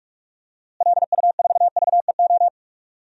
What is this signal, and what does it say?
Morse code, heard as a keyed tone, about 46 wpm, 700 Hz. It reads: LU4VEO
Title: DXCC-100 Call Signs - 6 Characters - Repeated 2x - 45wpm